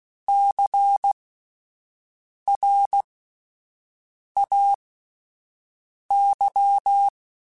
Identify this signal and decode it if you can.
Morse code, heard as a keyed tone, about 16 wpm, spaced as for 5 wpm, 800 Hz. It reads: CRAY
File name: Alphabetical / Alphabetical mp3s / Cs @5